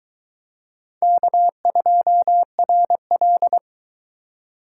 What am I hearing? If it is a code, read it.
Morse code, heard as a keyed tone, about 21 wpm, 700 Hz. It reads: K2RL